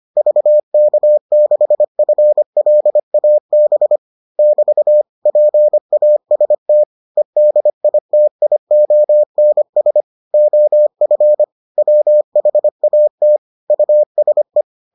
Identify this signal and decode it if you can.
Morse code, heard as a keyed tone, about 25 wpm, 600 Hz. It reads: VK6FLAB <BT> PAST EDITIONS OF WHAT USE